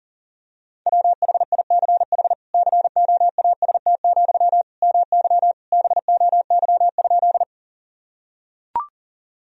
Morse code, heard as a keyed tone, about 40 wpm, 700 Hz. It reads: WHICH COAST, MY BOY?
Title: Sentences from Top 1000 Words - Review Part 4 - 40wpm